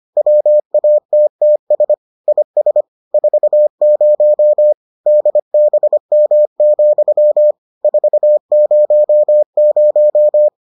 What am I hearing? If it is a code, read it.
Morse code, heard as a keyed tone, about 25 wpm, 600 Hz. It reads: WATTS IS 40 DBM, 400